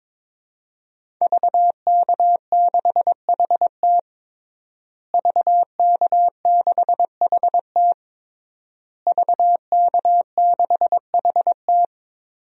Morse code, heard as a keyed tone, about 22 wpm, 700 Hz. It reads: VK6HT VK6HT VK6HT
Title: DXCC-100 Call Signs - 5 Characters - Repeated 3x - 22wpm